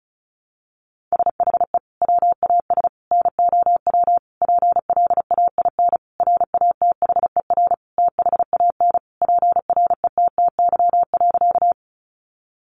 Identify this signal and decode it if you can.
Morse code, heard as a keyed tone, about 34 wpm, 700 Hz. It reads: SHE WAS NOW PLAIN RATHER THAN PRETTY.
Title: Sentences from Top 1000 Words 35wpm